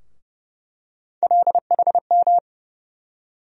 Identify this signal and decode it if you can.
Morse code, heard as a keyed tone, about 29 wpm, 700 Hz. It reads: LHM
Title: USA Call Sign Postfixes - 3 Letters 30wpm